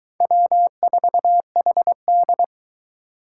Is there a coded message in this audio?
Yes: Morse code from 0.2 to 2.4 s, heard as a keyed tone, about 22 wpm, 700 Hz. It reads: W4HD